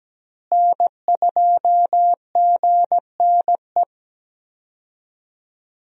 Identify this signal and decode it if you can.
Morse code, heard as a keyed tone, about 17 wpm, 700 Hz. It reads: N2GNE
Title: USA Call Signs - 5 Characters - Repeated 3x - 17wpm